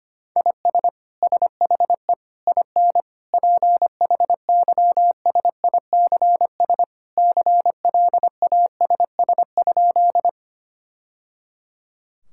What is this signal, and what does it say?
Morse code, heard as a keyed tone, about 25 wpm, 700 Hz. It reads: IS SHE IN PHYSICS CLASS?
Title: Sentences from Top 1000 Words - Review Part 5 - 25wpm